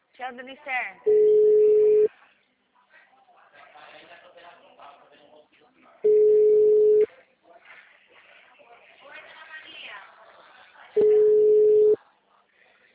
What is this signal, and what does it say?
Ringback tone (Europe, 425 Hz), with speech.
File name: out- -2012- - - .66194